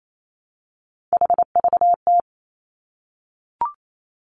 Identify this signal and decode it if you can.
Morse code, heard as a keyed tone, about 28 wpm, 700 Hz. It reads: HVT
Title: Three Challenging Letter-Number Combinations 28wpm